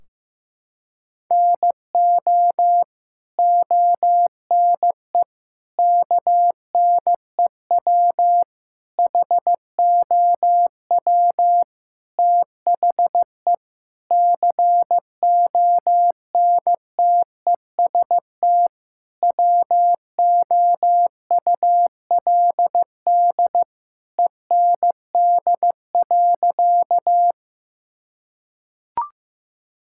Morse code, heard as a keyed tone, about 15 wpm, 700 Hz. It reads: NO ONE KNEW HOW THE CONTEST WOULD END.
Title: Sentences from Top 1600 Words 15wpm